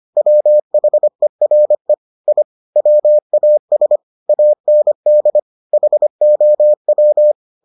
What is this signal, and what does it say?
Morse code, heard as a keyed tone, about 25 wpm, 600 Hz. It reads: WHERE I WAS AND HOW